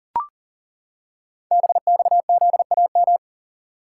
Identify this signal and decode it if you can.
Morse code, heard as a keyed tone, about 40 wpm, 700 Hz. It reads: BXZAM
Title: Code Groups - 5 Letters - 40wpm